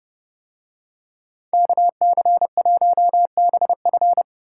Morse code, heard as a keyed tone, about 29 wpm, 700 Hz. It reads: KC1BF